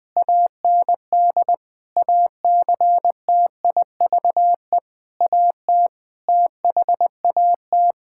Morse code, heard as a keyed tone, about 20 wpm, 700 Hz. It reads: AND ACTIVE AT THAT
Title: Sentences from Top 1000 Words - Review Part 6 - 20wpm